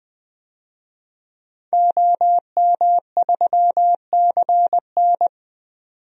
Morse code, heard as a keyed tone, about 21 wpm, 700 Hz. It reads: OM3CN
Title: DXCC-100 Call Signs - 5 Characters - 20wpm